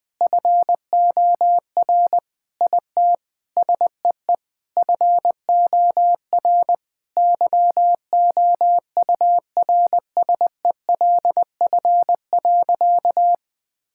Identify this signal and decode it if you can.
Morse code, heard as a keyed tone, about 21 wpm, 700 Hz. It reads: FOR IT SEE FOR YOURSELF.
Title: Sentences with 8-Word Phrases - 20wpm